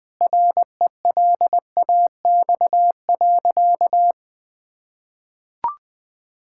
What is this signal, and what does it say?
Morse code, heard as a keyed tone, about 21 wpm, 700 Hz. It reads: RELAX.
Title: Sentences with 8-Word Phrases - 20wpm